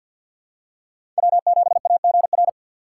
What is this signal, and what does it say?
Morse code, heard as a keyed tone, about 51 wpm, 690 Hz. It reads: W7AGR
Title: USA Call Signs - 5 Characters - 3x Character Spacing - 50wpm